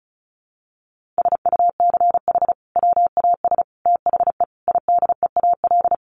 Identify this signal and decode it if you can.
Morse code, heard as a keyed tone, about 34 wpm, 700 Hz. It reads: SUCH WAS THE IDEAL
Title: Sentences from Top 1400 Words 35wpm